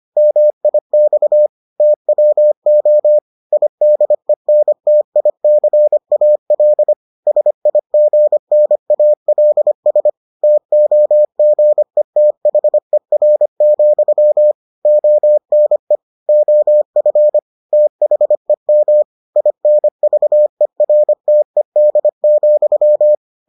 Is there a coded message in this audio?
Yes: Morse code from 0.2 to 23.1 s, heard as a keyed tone, about 25 wpm, 600 Hz. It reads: MIX TWO IDENTICAL SIGNALS TOGETHER, ONE OF THEM INVERTED,